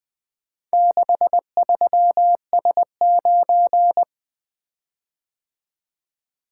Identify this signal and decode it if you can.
Morse code, heard as a keyed tone, about 25 wpm, 700 Hz. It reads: 63S9
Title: Speed-Racing - Four Letter-Number Combinations - 20wpm